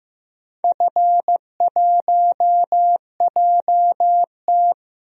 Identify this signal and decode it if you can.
Morse code, heard as a keyed tone, about 15 wpm, 700 Hz. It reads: F1JT
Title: DXCC-100 Call Signs - 4 Characters - 15wpm